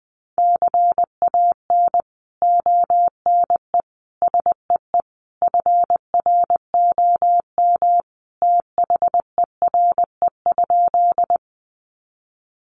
Morse code, heard as a keyed tone, about 20 wpm, 700 Hz. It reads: CAN ONE SEE FROM THERE?
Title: Sentences from Top 100 Words 20wpm